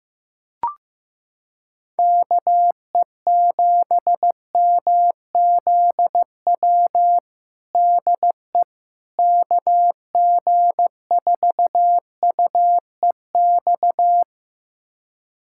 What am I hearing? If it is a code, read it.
Morse code, heard as a keyed tone, about 15 wpm, 700 Hz. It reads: KE7MZW DE KG4UEX
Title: USA Callsign DE Callsign - 6 Characters - 15wpm